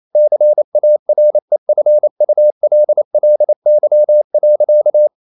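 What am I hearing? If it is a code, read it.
Morse code, heard as a keyed tone, about 28 wpm, 600 Hz. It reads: CAREFULLY.
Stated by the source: Morse code practice at 28 wpm, text of Ken Follett's novel Never in English, from CW med Gnister.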